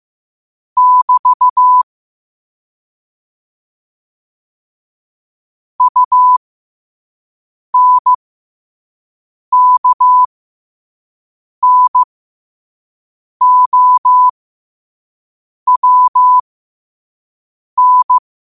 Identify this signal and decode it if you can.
Morse code, heard as a keyed tone, about 15 wpm, spaced as for 5 wpm, 1000 Hz. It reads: <BT> UNKNOWN
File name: qotdfiles / 5wpm